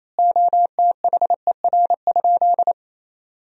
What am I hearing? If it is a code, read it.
Morse code, heard as a keyed tone, about 28 wpm, 700 Hz. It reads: OTHER?
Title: 4x Word-Spacing - Sentences from Top 200 Words - 28wpm